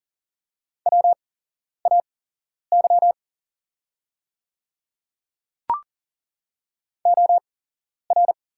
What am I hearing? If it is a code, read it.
Morse code, heard as a keyed tone, about 40 wpm, spaced as for 11 wpm, 700 Hz. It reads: WAY OR
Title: 8x Character Spacing - Top 200 Words - 40wpm